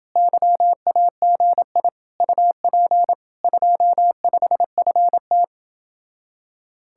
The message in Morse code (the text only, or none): YAGI UP 25FT